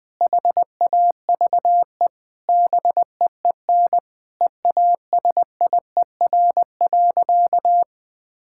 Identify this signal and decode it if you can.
Morse code, heard as a keyed tone, about 20 wpm, 700 Hz. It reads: HAVE BEEN EASIER.
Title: Sentences with 3-Word Phrases - Spoken Beginning - 20wpm